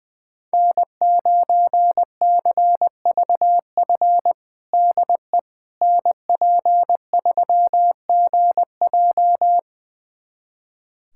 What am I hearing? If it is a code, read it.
Morse code, heard as a keyed tone, about 20 wpm, 700 Hz. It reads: N9CVF DE NP3GJ
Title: USA Callsign DE Callsign - 5 Characters - 20wpm